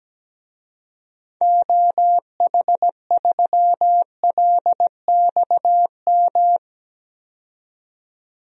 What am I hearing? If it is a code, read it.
Morse code, heard as a keyed tone, about 17 wpm, 700 Hz. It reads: OH3LXM
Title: DXCC-100 Call Signs - 6 Characters - 17wpm